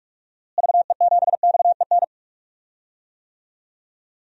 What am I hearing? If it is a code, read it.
Morse code, heard as a keyed tone, about 44 wpm, 700 Hz. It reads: VE7XEN
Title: DXCC-100 Call Signs - 6 Characters - Repeated 2x - 45wpm